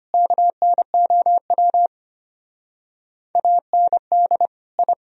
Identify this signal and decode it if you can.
Morse code, heard as a keyed tone, about 32 wpm, 700 Hz. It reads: KNOW AND I